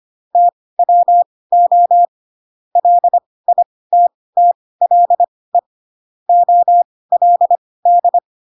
Morse code, heard as a keyed tone, about 25 wpm, spaced as for 18 wpm, 700 Hz. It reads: TWO LITTLE OLD
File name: Pinocchio / Pinocchio ch2 12 25 ik1hge